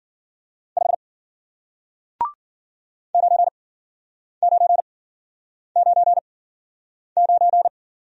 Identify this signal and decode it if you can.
Morse code, heard as a keyed tone, about 50 wpm, spaced as for 9 wpm, 700 Hz. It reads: 5 9999